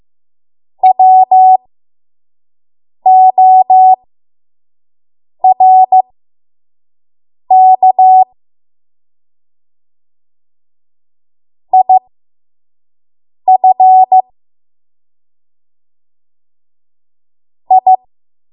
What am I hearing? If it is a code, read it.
Morse code, heard as a keyed tone, about 15 wpm, spaced as for 5 wpm, 750 Hz. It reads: WORK IF I
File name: Morse / 05WPM